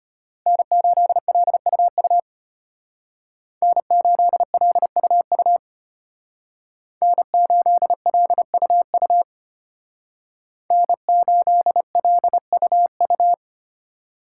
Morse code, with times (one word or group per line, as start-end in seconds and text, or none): 0.5-2.2 N8LUU
3.6-5.6 N8LUU
7.0-9.2 N8LUU
10.7-13.3 N8LUU